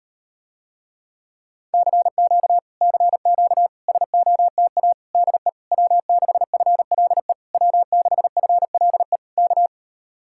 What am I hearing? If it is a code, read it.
Morse code, heard as a keyed tone, about 38 wpm, 700 Hz. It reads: CQ CQ SOTA DE W6FLE W6FLE K